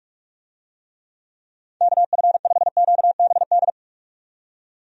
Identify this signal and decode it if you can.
Morse code, heard as a keyed tone, about 47 wpm, 700 Hz. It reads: KF5QBD
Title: USA Call Signs - 6 Characters - 45wpm